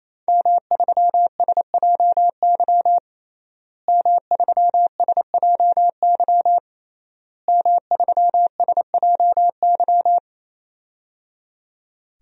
Morse code, heard as a keyed tone, about 28 wpm, 700 Hz. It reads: M3SJY M3SJY M3SJY
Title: DXCC-100 Call Signs - 5 Characters - Repeated 3x - 28wpm